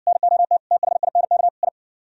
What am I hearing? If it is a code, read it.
Morse code, heard as a keyed tone, about 59 wpm, 700 Hz. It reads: NOT THINK I